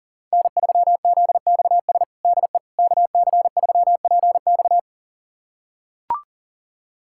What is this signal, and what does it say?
Morse code, heard as a keyed tone, about 40 wpm, 700 Hz. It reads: N3ZXS DE KC3PX
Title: USA Callsign DE Callsign - 5 Characters - 40wpm